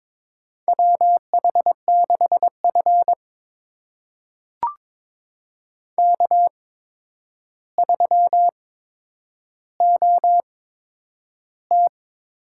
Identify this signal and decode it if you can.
Morse code, heard as a keyed tone, about 22 wpm, spaced as for 9 wpm, 700 Hz. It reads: WH6F K3OT
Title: USA Call Signs - 8x Character Spacing - 22wpm